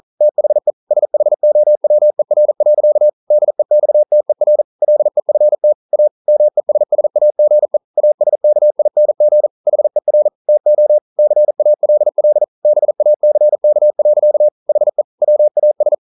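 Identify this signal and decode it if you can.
Morse code, heard as a keyed tone, about 42 wpm, 600 Hz. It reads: THE SHOWER. DEXTER LEFT A MESSAGE ASKING HER TO CALL BACK. HE WAS
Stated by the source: Morse code practice at 41 wpm, from CW med Gnister.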